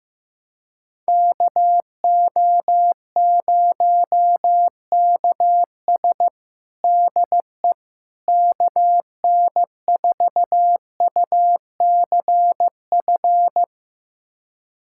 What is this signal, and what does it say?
Morse code, heard as a keyed tone, about 15 wpm, 700 Hz. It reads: KO0KS DE KN4UCF
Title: USA Callsign DE Callsign 15wpm